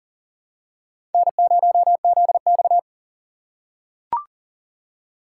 Morse code, heard as a keyed tone, about 40 wpm, 700 Hz. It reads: N0ZX
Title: USA Call Signs - 4 Characters - 6x Character Spacing - 40wpm